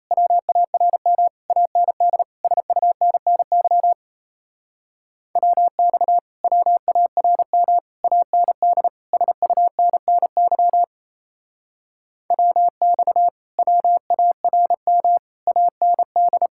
Morse code, heard as a keyed tone, about 32 wpm, 700 Hz. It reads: WARM AND SUNNY WX WARM AND SUNNY WX WARM AND
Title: Speed-Racing Common QSO Phrases Encore - 28wpm